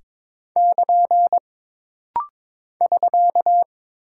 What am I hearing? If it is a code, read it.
Morse code, heard as a keyed tone, about 22 wpm, 700 Hz. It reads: <KN> <SK>
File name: practice / 22wpm / Single Punctuation-Prosign - Rapid-Fire - 22wpm